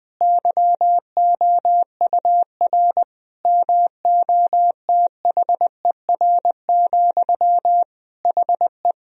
Morse code, heard as a keyed tone, about 20 wpm, 700 Hz. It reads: YOUR MOTHER, HE
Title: Sentences from Top 1000 Words - Review Part 4 - 20wpm